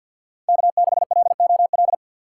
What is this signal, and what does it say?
Morse code, heard as a keyed tone, about 52 wpm, 700 Hz. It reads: K6LOL
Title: USA Call Signs - 5 Characters - 4x Character Spacing - 50wpm